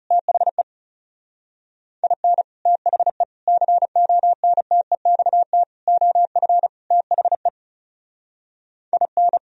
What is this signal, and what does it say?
Morse code, heard as a keyed tone, about 37 wpm, 700 Hz. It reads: THE IN THE CONTEXT OF THE IN
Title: Speed-Racing - Sets of 5 Words 30wpm